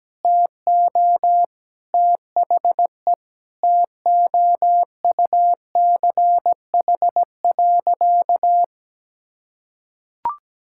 Morse code, heard as a keyed tone, about 17 wpm, 700 Hz. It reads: TO THE TOUCH.
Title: Sentences from Top 1600 Words 17wpm